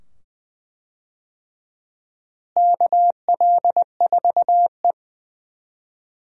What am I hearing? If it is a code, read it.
Morse code, heard as a keyed tone, about 20 wpm, 700 Hz. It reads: KL4E
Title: USA Call Signs - 4 Characters - Repeated 2x - 20wpm